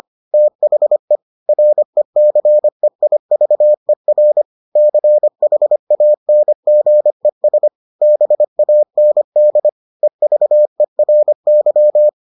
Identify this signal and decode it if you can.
Morse code, heard as a keyed tone, about 25 wpm, 600 Hz. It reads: THE RECEIVER CHANGES BAND EVERY